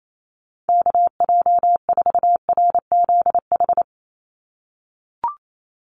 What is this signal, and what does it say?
Morse code, heard as a keyed tone, about 28 wpm, 700 Hz. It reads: KJ4RZH
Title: USA Call Signs 28wpm